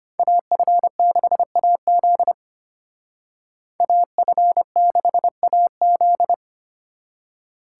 Morse code, heard as a keyed tone, about 32 wpm, 700 Hz. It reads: AF6AZ AF6AZ